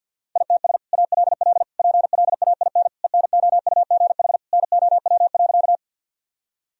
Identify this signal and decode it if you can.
Morse code, heard as a keyed tone, about 51 wpm, 700 Hz. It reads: ITS ALL PLAIN ENOUGH NOW.
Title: Sentences from Top 1000 Words - Review Part 2 - 50wpm